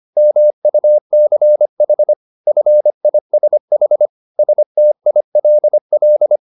Morse code, heard as a keyed tone, about 25 wpm, 600 Hz. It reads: MUCH FISH STILL